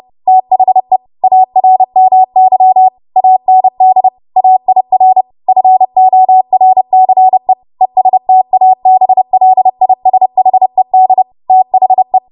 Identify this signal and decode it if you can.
Morse code, heard as a keyed tone, about 30 wpm, 750 Hz. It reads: THE ARMY AND AIR FORCE ESTABLISHED THE
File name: Archive / 30 WPM / 30WPM